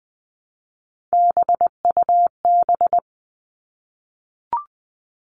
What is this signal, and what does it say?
Morse code, heard as a keyed tone, about 20 wpm, 700 Hz. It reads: BUB
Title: All Cut Numbers - 3-Digit 20wpm